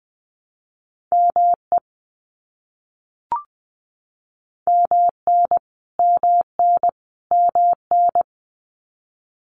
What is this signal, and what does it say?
Morse code, heard as a keyed tone, about 20 wpm, 700 Hz. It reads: ME MN MN MN
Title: US States Repeated 3x 20wpm